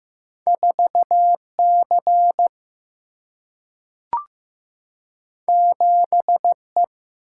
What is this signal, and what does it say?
Morse code, heard as a keyed tone, about 15 wpm, 700 Hz. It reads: 4C 7E
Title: Single-Letter Single-Number 15wpm